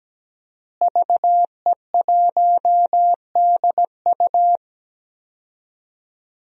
Morse code, heard as a keyed tone, about 17 wpm, 700 Hz. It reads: VE1DU
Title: DXCC-100 Call Signs - 5 Characters - Repeated 2x - 17wpm